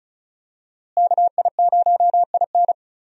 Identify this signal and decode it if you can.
Morse code, heard as a keyed tone, about 34 wpm, 700 Hz. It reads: KI0IN
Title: USA Call Signs - 5 Characters - Repeated 2x - 35wpm